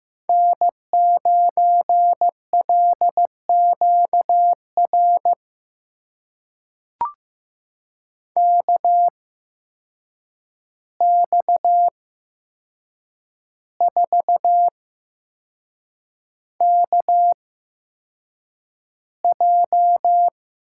Morse code, heard as a keyed tone, about 15 wpm, spaced as for 6 wpm, 700 Hz. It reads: N9LQR KX4KJ
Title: USA Call Signs - 5 Characters - 8x Character Spacing - 15wpm